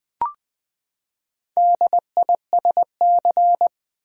Morse code, heard as a keyed tone, about 20 wpm, 700 Hz. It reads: DISC
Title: Build A Word - 4 Letters At A Time - 8-Letter Words - 20wpm